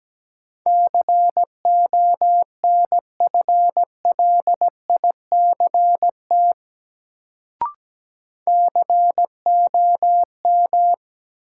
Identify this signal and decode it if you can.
Morse code, heard as a keyed tone, about 17 wpm, 700 Hz. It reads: CONFLICT COM
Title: Sentences from Top 1300 Words - New Word Drill Alt - 17wpm